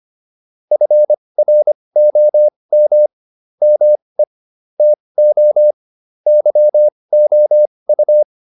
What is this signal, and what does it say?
Morse code, heard as a keyed tone, about 25 wpm, spaced as for 20 wpm, 600 Hz. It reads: FROM ME TO YOU
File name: PR302 20